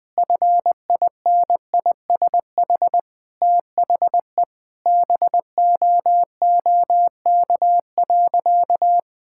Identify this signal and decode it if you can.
Morse code, heard as a keyed tone, about 20 wpm, 700 Hz. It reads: FINISH THE BOOK.
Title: Sentences with 7-Word Phrases - 20wpm